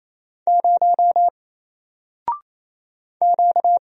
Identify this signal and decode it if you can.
Morse code, heard as a keyed tone, about 27 wpm, 700 Hz. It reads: 0 Q
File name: Single-Letter-Number-28wpm